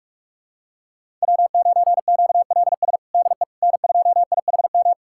Morse code, heard as a keyed tone, about 42 wpm, 700 Hz. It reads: W9QLS DE N2IHM
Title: USA Callsign DE Callsign - 5 Characters - 45wpm